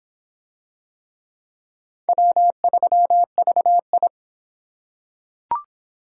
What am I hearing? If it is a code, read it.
Morse code, heard as a keyed tone, about 22 wpm, 700 Hz. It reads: W3VI